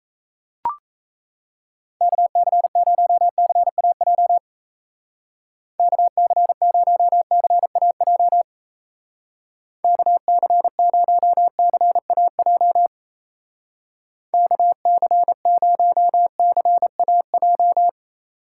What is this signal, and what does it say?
Morse code, heard as a keyed tone, about 35 wpm, 700 Hz. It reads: KC0CAJ KC0CAJ KC0CAJ KC0CAJ
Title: Speed Racing - USA 2x3 Call Signs 28wpm